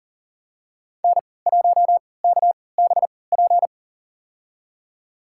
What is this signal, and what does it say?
Morse code, heard as a keyed tone, about 37 wpm, spaced as for 22 wpm, 700 Hz. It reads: N1KBP
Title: USA Call Signs - 5 Characters - 3x Character Spacing - 40wpm